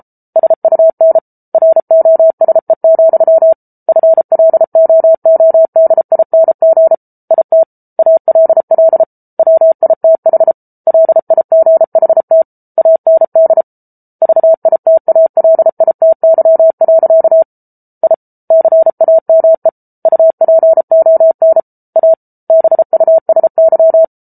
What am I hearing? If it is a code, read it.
Morse code, heard as a keyed tone, about 34 wpm, 650 Hz. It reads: SUN ROSE, FLOODING IT ALL WITH LIGHT AND VITALITY. I CAME UPON A BUSY